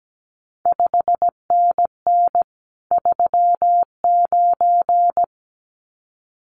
Morse code, heard as a keyed tone, about 17 wpm, 700 Hz. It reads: 5NN 39
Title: CQ WW DX Contest Exchange - 17wpm